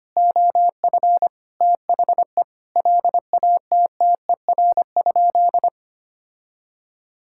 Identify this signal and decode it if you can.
Morse code, heard as a keyed tone, about 25 wpm, 700 Hz. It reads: OF THE LATTER?
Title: Sentences from Top 1600 Words 25wpm